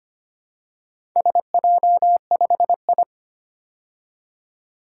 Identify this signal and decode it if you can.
Morse code, heard as a keyed tone, about 25 wpm, 700 Hz. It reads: SJ5I
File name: practice / 25wpm / Four Letter-Number Combinations - 25wpm